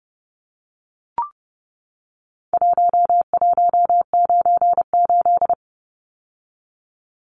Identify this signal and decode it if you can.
Morse code, heard as a keyed tone, about 30 wpm, 700 Hz. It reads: 1198
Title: Numbers - Contest Style - 4-Digit 30wpm